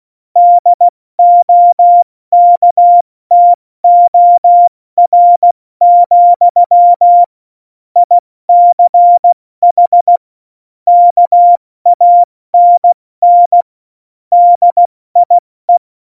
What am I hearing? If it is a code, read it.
Morse code, heard as a keyed tone, about 16 wpm, 700 Hz. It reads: DOKTOR, ICH KANN DIE
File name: Morsekurs / Witze / Witz 30. Januar 2018, 70 F80